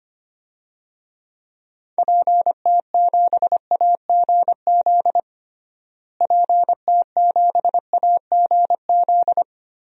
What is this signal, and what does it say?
Morse code, heard as a keyed tone, about 25 wpm, 700 Hz. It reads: PT7AGZ PT7AGZ
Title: DXCC-100 Call Signs - 6 Characters - Repeated 3x - 25wpm